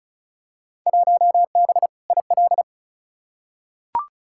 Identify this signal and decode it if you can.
Morse code, heard as a keyed tone, about 34 wpm, 700 Hz. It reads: 1B IL
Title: Field Day Exchanges - 35wpm